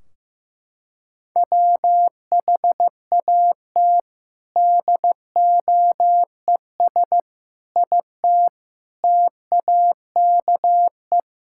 Morse code, heard as a keyed tone, about 15 wpm, 700 Hz. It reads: WHAT DOES IT TAKE
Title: Sentences from Top 1600 Words 15wpm